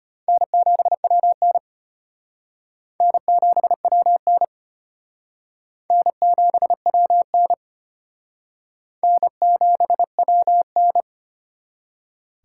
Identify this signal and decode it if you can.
Morse code, heard as a keyed tone, about 31 wpm, 700 Hz. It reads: N7WN N7WN N7WN N7WN
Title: Speed Racing - USA 1x2 Call Signs 25wpm